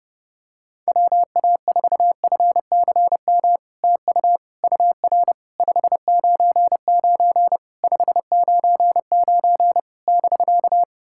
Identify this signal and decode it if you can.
Morse code, heard as a keyed tone, about 30 wpm, 700 Hz. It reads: WA4FCM TU UR 599 599 <BK>
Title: SOTA QSO Practice 30wpm